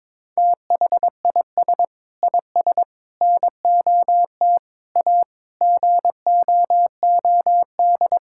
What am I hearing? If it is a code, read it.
Morse code, heard as a keyed tone, about 22 wpm, 700 Hz. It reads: THIS IS NOT A GOOD
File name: practice / 22wpm / Sentences with 4-Word Phrases - 22wpm